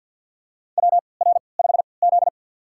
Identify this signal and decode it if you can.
Morse code, heard as a keyed tone, about 46 wpm, spaced as for 27 wpm, 700 Hz. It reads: WR5Z
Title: USA Call Signs - 3x Character Spacing - 50wpm